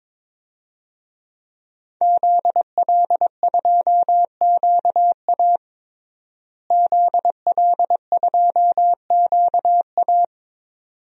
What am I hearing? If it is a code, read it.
Morse code, heard as a keyed tone, about 22 wpm, 700 Hz. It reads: ZL2QA ZL2QA
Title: DXCC-100 Call Signs - 5 Characters - Repeated 2x - 22wpm